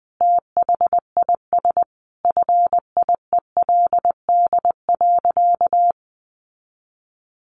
Morse code, heard as a keyed tone, about 20 wpm, 700 Hz. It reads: THIS FIELD.
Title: Sentences from Top 1200 Words 20wpm